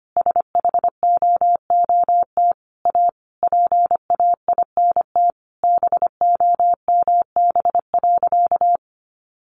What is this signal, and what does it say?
Morse code, heard as a keyed tone, about 25 wpm, 700 Hz. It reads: SHOOT A PAINT BOMB.
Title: A Sound Of Thunder 25wpm